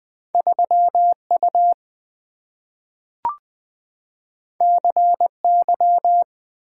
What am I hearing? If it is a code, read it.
Morse code, heard as a keyed tone, about 20 wpm, 700 Hz. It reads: 3U CY
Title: Two Letter-Number Combinations - 20wpm